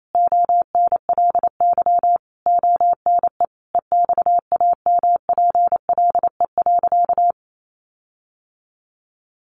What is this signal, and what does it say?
Morse code, heard as a keyed tone, about 28 wpm, 700 Hz. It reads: ONLY ONE EXAMPLE.
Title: Sentences from Top 1500 Words 28wpm